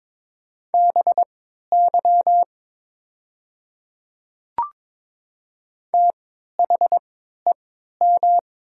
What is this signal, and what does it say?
Morse code, heard as a keyed tone, about 22 wpm, spaced as for 12 wpm, 700 Hz. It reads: BY THEM
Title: 3x Character Spacing - Top 100 Words - 22wpm